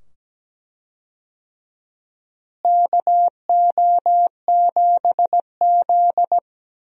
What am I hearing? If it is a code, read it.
Morse code, heard as a keyed tone, about 17 wpm, 700 Hz. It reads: KO7Z
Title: USA Call Signs - 4 Characters - Repeated 2x - 17wpm